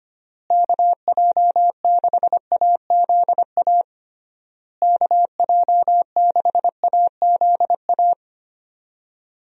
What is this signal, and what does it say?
Morse code, heard as a keyed tone, about 25 wpm, 700 Hz. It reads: KJ6AZA KJ6AZA
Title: USA Call Signs - 6 Characters - Repeated 3x - 25wpm